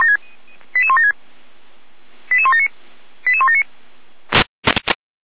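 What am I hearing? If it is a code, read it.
Five-tone selective call (ZVEI), heard as quick runs of five tones, tones of about 70 ms. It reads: 88176 88179 88179
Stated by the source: Five-tone ZVEI selective call 88122 received by SDR.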